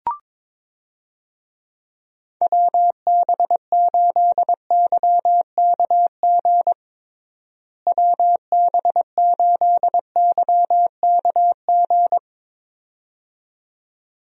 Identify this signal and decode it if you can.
Morse code, heard as a keyed tone, about 22 wpm, 700 Hz. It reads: WB8YKG WB8YKG
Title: USA Call Signs - 6 Characters - Repeated 2x - 22wpm